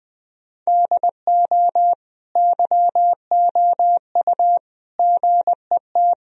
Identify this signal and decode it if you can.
Morse code, heard as a keyed tone, about 21 wpm, 700 Hz. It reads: DO YOU GET
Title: Sentences from Top 1000 Words - Review Part 4 - 20wpm